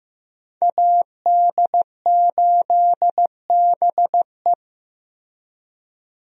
Morse code, heard as a keyed tone, about 15 wpm, 700 Hz. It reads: AD8BE
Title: USA Call Signs - 5 Characters - Repeated 2x - 15wpm